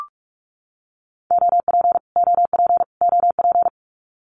Morse code, heard as a keyed tone, about 44 wpm, 700 Hz. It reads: OP OP OP
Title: Abbreviations Repeated 3x - 45wpm